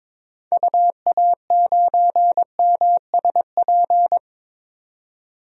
Morse code, heard as a keyed tone, about 22 wpm, 700 Hz. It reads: UA9MSP
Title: DXCC-100 Call Signs - 6 Characters - 22wpm